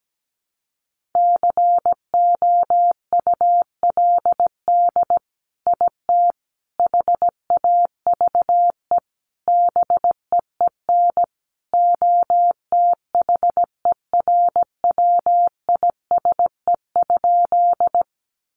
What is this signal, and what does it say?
Morse code, heard as a keyed tone, about 17 wpm, 700 Hz. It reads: COULD IT HAVE BEEN OTHERWISE?
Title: Sentences from Top 1200 Words 17wpm